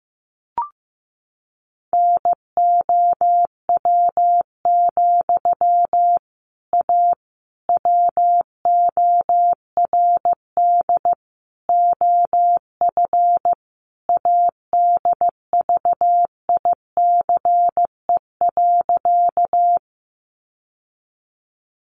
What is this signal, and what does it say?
Morse code, heard as a keyed tone, about 15 wpm, 700 Hz. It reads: NOW, A WORD OF ADVICE.
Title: Sentences from Top 1100 Words 15wpm